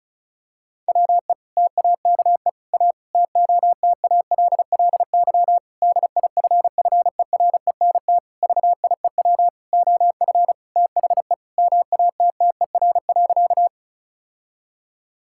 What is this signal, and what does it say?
Morse code, heard as a keyed tone, about 36 wpm, 700 Hz. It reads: WE TAKE A TOTALLY DIFFERENT VIEW OF THE MATTER.
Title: Sentences from Top 1600 Words 35wpm